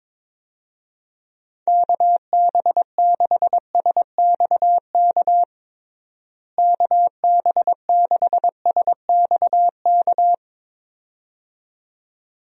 Morse code, heard as a keyed tone, about 22 wpm, 700 Hz. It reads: KB6SXK KB6SXK
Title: USA Call Signs - 6 Characters - Repeated 2x - 22wpm